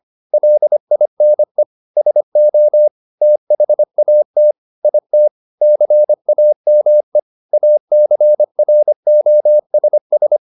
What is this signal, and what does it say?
Morse code, heard as a keyed tone, about 25 wpm, 600 Hz. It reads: LINE SO THAT IT CAME ACROSS
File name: the old man and the sea part 1 25wpm